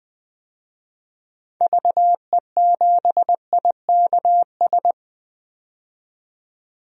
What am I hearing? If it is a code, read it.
Morse code, heard as a keyed tone, about 20 wpm, 700 Hz. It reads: VE7IKS
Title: DXCC-100 Call Signs - 6 Characters - 20wpm